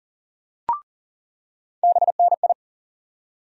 Morse code, heard as a keyed tone, about 39 wpm, 700 Hz. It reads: BNI
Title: USA Call Sign Postfixes - 3 Letters 40wpm